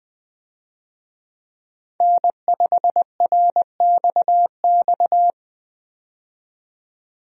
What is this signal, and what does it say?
Morse code, heard as a keyed tone, about 20 wpm, 700 Hz. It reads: N5RXX
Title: USA Call Signs - 5 Characters - 20wpm